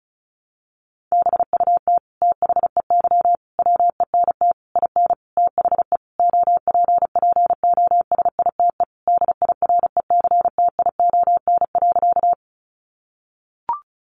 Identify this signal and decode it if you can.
Morse code, heard as a keyed tone, about 34 wpm, 700 Hz. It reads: BUT THEY WENT IN THE OPPOSITE DIRECTION.
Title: Sentences from Top 1300 Words 35wpm